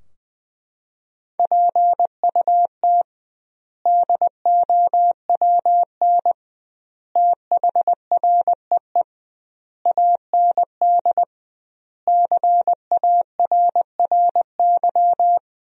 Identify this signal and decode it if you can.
Morse code, heard as a keyed tone, about 21 wpm, 700 Hz. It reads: PUT DOWN THREE AND CARRY
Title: 2x Word-Spacing - Sentences from Top 400 Words - 20wpm